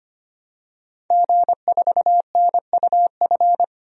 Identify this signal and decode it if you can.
Morse code, heard as a keyed tone, about 25 wpm, 700 Hz. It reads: G4NUF
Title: DXCC-100 Call Signs - 5 Characters - 25wpm